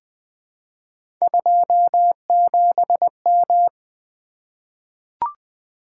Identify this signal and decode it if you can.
Morse code, heard as a keyed tone, about 20 wpm, 700 Hz. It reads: 27M
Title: Three Letter-Number Combinations - 20wpm